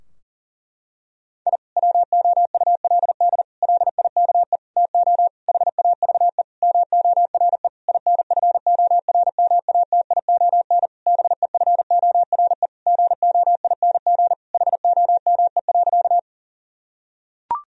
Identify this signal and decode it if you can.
Morse code, heard as a keyed tone, about 40 wpm, 700 Hz. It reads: I WOULD LIKE TO HAVE MORE INFORMATION BEFORE GOING HOME.
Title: Sentences from Top 1000 Words - Review Part 3 - 40wpm